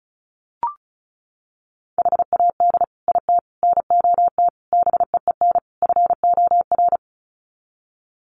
Morse code, heard as a keyed tone, about 34 wpm, 700 Hz. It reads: HAD IT NOT BEEN FOR
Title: Sets of 5 Words Encore2 35wpm